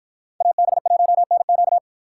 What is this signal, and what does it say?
Morse code, heard as a keyed tone, about 51 wpm, 700 Hz. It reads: AB1NQ